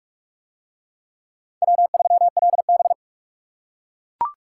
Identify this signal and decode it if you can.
Morse code, heard as a keyed tone, about 44 wpm, 700 Hz. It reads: W3LB